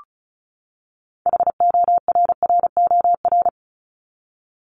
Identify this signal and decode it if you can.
Morse code, heard as a keyed tone, about 34 wpm, 700 Hz. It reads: HORROR
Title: Top 1501-1750 Words 35wpm